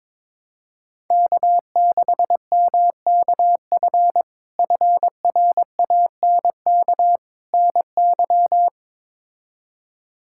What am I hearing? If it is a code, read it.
Morse code, heard as a keyed tone, about 22 wpm, 700 Hz. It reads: K6MKF FRANK NY
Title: K1USN SST QSO Practice - 22wpm